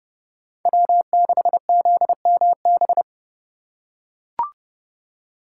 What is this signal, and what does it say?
Morse code, heard as a keyed tone, about 29 wpm, 700 Hz. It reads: W6ZMB